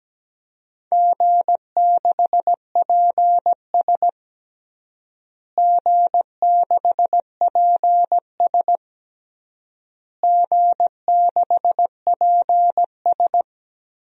Morse code, heard as a keyed tone, about 17 wpm, 700 Hz. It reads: G6PS G6PS G6PS
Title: DXCC-100 Call Signs - 4 Characters - Repeated 3x - 17wpm